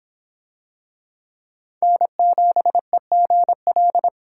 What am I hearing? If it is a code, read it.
Morse code, heard as a keyed tone, about 25 wpm, 700 Hz. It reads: N7EGL